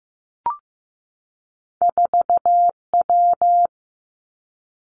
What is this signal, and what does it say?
Morse code, heard as a keyed tone, about 15 wpm, 700 Hz. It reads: 4W
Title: Two Challenging Letter-Number Combinations 15wpm